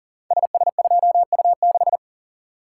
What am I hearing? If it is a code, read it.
Morse code, heard as a keyed tone, about 42 wpm, 700 Hz. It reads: SS2U6
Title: Code Groups - 5 Letter:Numbers - 40wpm